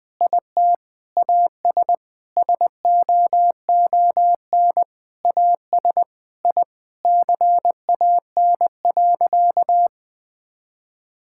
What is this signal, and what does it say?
Morse code, heard as a keyed tone, about 20 wpm, 700 Hz. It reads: IT AS SOON AS I CAN.
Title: Sentences from Top 1000 Words - Review Part 5 - 20wpm